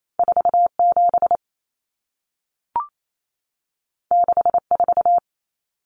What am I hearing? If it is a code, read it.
Morse code, heard as a keyed tone, about 28 wpm, 700 Hz. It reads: 47 64
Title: Numbers - 2-Digit 28wpm